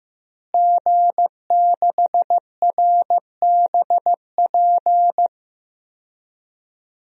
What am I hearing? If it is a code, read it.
Morse code, heard as a keyed tone, about 15 wpm, 700 Hz. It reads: G6RBP
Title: DXCC-100 Call Signs - 5 Characters - Repeated 2x - 15wpm